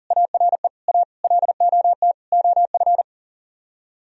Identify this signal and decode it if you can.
Morse code, heard as a keyed tone, about 51 wpm, 700 Hz. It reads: ARE A LOT OF